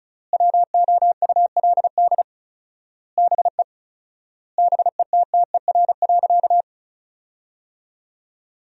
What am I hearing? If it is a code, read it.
Morse code, heard as a keyed tone, about 34 wpm, 700 Hz. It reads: WOULD BE BETTER.
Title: 4x Word-Spacing - Sentences from Top 400 Words - 35wpm